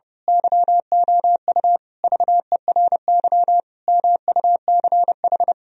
Morse code, heard as a keyed tone, about 30 wpm, 700 Hz. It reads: YOU VERY MUCH
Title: Speed-Racing - Sets of 5 Words 30wpm